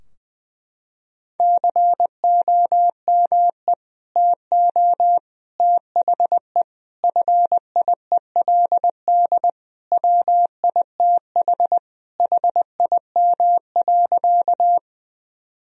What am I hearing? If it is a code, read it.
Morse code, heard as a keyed tone, about 19 wpm, 700 Hz. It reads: COME TO THE FIELD WITH HIM.
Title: Sentences from Top 1000 Words - Review Part 2 - 20wpm